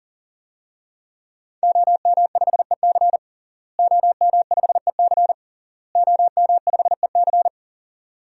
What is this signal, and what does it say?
Morse code, heard as a keyed tone, about 40 wpm, 700 Hz. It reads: OM5EC OM5EC OM5EC
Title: DXCC-100 Call Signs - 5 Characters - Repeated 3x - 40wpm